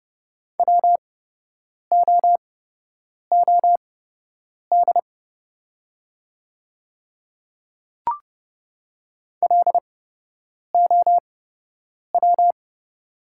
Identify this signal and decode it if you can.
Morse code, heard as a keyed tone, about 30 wpm, spaced as for 8 wpm, 690 Hz. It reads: WOOD LOW
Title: 8x Character Spacing - Top 500 Words - 30wpm